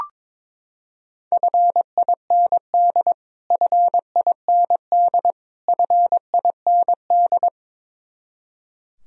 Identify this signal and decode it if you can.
Morse code, heard as a keyed tone, about 22 wpm, 700 Hz. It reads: FIND FIND FIND
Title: Top 100 Words Repeated 22wpm